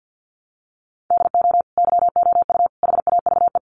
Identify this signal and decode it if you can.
Morse code, heard as a keyed tone, about 50 wpm, 700 Hz. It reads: DO YOU HAVE
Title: Sets of 3 Words 50wpm